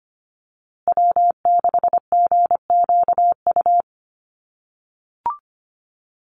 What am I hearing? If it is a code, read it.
Morse code, heard as a keyed tone, about 25 wpm, 700 Hz. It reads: W6GQU
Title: USA Call Signs Repeated 2x 25wpm